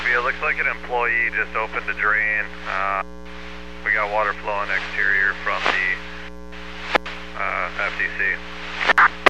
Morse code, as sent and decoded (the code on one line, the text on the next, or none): none